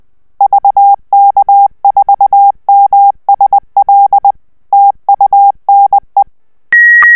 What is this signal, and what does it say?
Morse code, heard as a keyed tone, about 20 wpm, 800 Hz. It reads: VK4MSL TUNE